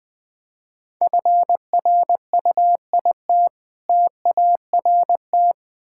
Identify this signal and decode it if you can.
Morse code, heard as a keyed tone, about 20 wpm, 690 Hz. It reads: FRUIT TART
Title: Difficult Words with Hints - 20wpm